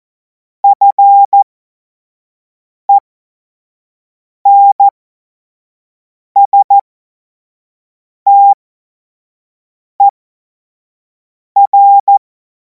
Morse code, heard as a keyed tone, about 14 wpm, spaced as for 5 wpm, 800 Hz. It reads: FENSTER
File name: Morsekurs / Witze / Witz 10. Februar 2026 25 F70